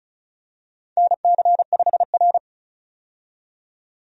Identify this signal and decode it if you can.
Morse code, heard as a keyed tone, about 36 wpm, 700 Hz. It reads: NC5R
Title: USA Call Signs - 4 Characters - 35wpm